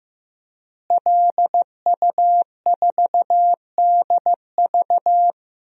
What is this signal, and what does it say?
Morse code, heard as a keyed tone, about 16 wpm, 700 Hz. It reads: LU4DV
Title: DXCC-100 Call Signs - 5 Characters - Repeated 2x - 15wpm